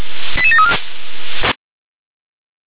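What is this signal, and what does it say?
Five-tone selective call (ZVEI), heard as quick runs of five tones, tones of about 70 ms. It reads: 88734
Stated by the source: ZVEI five-tone selective call 88148 received on SDR.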